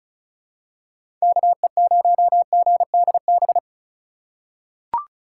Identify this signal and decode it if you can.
Morse code, heard as a keyed tone, about 36 wpm, 700 Hz. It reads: KE0GDB
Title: USA Call Signs - 6 Characters - 8x Character Spacing - 35wpm